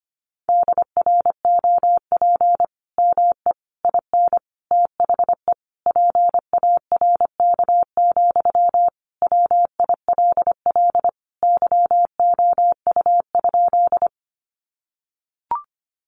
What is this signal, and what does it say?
Morse code, heard as a keyed tone, about 25 wpm, 700 Hz. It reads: DROP ME IN THE PARK, WILL YOU?
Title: Sentences from Top 1400 Words 25wpm